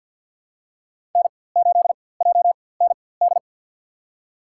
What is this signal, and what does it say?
Morse code, heard as a keyed tone, about 48 wpm, spaced as for 23 wpm, 690 Hz. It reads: N8JND
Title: USA Call Signs - 5 Characters - 4x Character Spacing - 50wpm